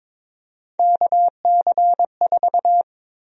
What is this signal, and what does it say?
Morse code, heard as a keyed tone, about 22 wpm, 700 Hz. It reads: KC4
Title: USA Call Sign Prefixes - 22wpm